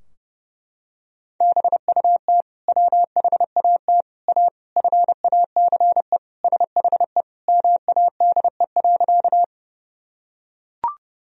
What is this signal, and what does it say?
Morse code, heard as a keyed tone, about 30 wpm, 700 Hz. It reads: BUT WHAT A FACE SHE MADE.
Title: Sentences from Top 1000 Words - Review Part 2 - 30wpm